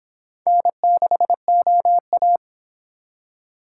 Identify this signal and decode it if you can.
Morse code, heard as a keyed tone, about 25 wpm, 700 Hz. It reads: N6OA